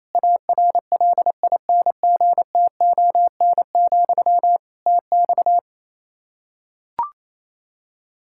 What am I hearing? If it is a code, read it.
Morse code, heard as a keyed tone, about 28 wpm, 700 Hz. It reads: ARLINGTON, TX
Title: Practice QSO - Exchange 1 - Beginning - 28wpm